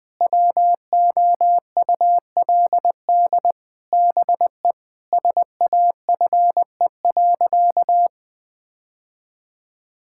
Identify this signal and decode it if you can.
Morse code, heard as a keyed tone, about 20 wpm, 700 Hz. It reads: WOULD BE SAFE.
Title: Sentences from Top 1600 Words 20wpm